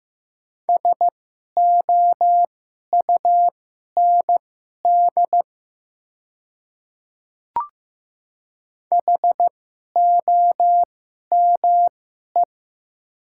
Morse code, heard as a keyed tone, about 15 wpm, spaced as for 11 wpm, 700 Hz. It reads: SOUND HOME
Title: 2x Character Spacing - Top 300 Words - 15wpm